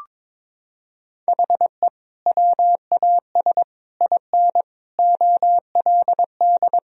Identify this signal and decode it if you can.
Morse code, heard as a keyed tone, about 22 wpm, 700 Hz. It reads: HE WAS IN OLD
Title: Sentences from Top 1000 Words - Review Part 3 - 22wpm